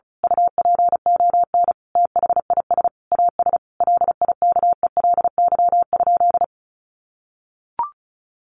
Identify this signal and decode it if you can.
Morse code, heard as a keyed tone, about 34 wpm, 700 Hz. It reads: UPON THIS AS LIKELY?
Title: Sentences from Top 1100 Words 35wpm